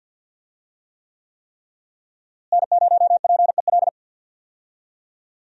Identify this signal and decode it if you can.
Morse code, heard as a keyed tone, about 46 wpm, 700 Hz. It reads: N0PEL
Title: USA Call Signs - 5 Characters - Repeated 3x - 50wpm